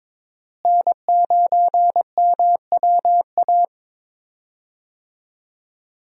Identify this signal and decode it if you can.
Morse code, heard as a keyed tone, about 22 wpm, 700 Hz. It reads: N9MWA
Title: USA Call Signs - 5 Characters - Repeated 3x - 22wpm